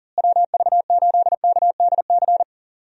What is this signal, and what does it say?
Morse code, heard as a keyed tone, about 40 wpm, 700 Hz. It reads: WV8KDC
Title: USA Call Signs - 6 Characters - Repeated 2x - 40wpm